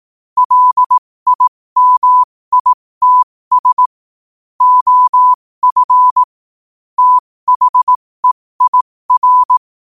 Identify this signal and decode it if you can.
Morse code, heard as a keyed tone, about 18 wpm, spaced as for 15 wpm, 1000 Hz. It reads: LIMITS OF THEIR
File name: 15wpm